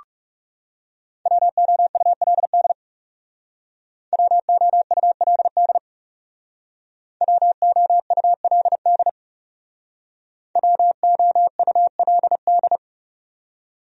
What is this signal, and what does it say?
Morse code, heard as a keyed tone, about 40 wpm, 700 Hz. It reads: WOULD WOULD WOULD WOULD